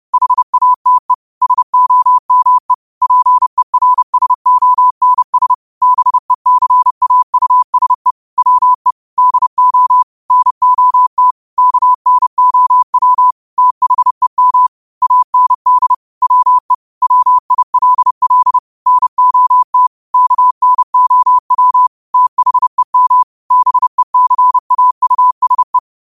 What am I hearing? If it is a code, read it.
Morse code, heard as a keyed tone, about 30 wpm, 1000 Hz. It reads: HATE SOME PERSONS BECAUSE WE DO NOT KNOW THEM AND WE WILL NOT KNOW THEM BECAUSE
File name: qotdfiles / 30wpm